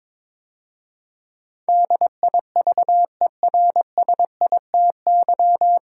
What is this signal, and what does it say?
Morse code, heard as a keyed tone, about 22 wpm, 700 Hz. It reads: DIVERSITY
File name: 9 Letter Words - First 3 Letters Given - 22wpm